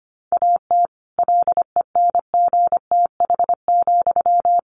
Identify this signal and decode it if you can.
Morse code, heard as a keyed tone, about 25 wpm, 700 Hz. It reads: AT LENGTH,
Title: Sentences from Top 1000 Words 25wpm